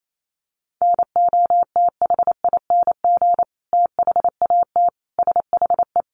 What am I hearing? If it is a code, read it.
Morse code, heard as a keyed tone, about 28 wpm, 700 Hz. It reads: NOTHING THAT SHE
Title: Sentences from Top 1100 Words 28wpm